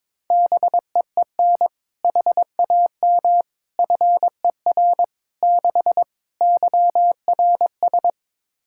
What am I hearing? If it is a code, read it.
Morse code, heard as a keyed tone, about 22 wpm, 700 Hz. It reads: BEEN HAM FER 6 YRS
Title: Practice QSO - Exchange 3 - More Info - 22wpm